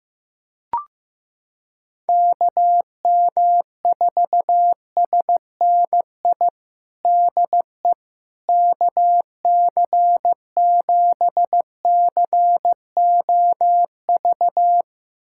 Morse code, heard as a keyed tone, about 15 wpm, 700 Hz. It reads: KM4SNI DE KC7COV
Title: USA Callsign DE Callsign 15wpm